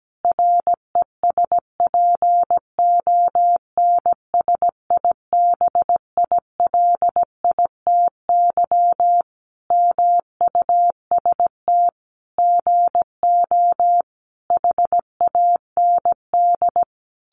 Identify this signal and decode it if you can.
Morse code, heard as a keyed tone, about 17 wpm, 700 Hz. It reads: RESPONSIBILITY MUST GO HAND
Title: Sentences from Top 1200 Words 17wpm